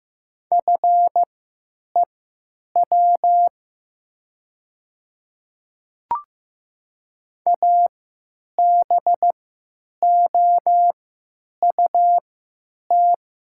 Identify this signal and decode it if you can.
Morse code, heard as a keyed tone, about 15 wpm, spaced as for 9 wpm, 700 Hz. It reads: FEW ABOUT
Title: 3x Character Spacing - Top 300 Words - 15wpm